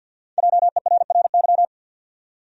Morse code, heard as a keyed tone, about 52 wpm, 700 Hz. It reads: JERRY
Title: USA Common Names - Max 5 Letters - 50wpm